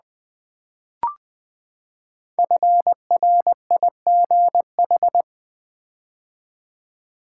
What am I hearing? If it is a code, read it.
Morse code, heard as a keyed tone, about 20 wpm, 700 Hz. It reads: FRIGH
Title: Build A Word - 5 Letters At A Time - 10-Letter Words - 20wpm